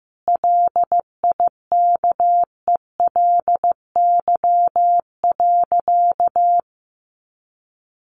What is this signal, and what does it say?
Morse code, heard as a keyed tone, about 15 wpm, 700 Hz. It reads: LIKELY.
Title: Sentences from Top 600 Words 15wpm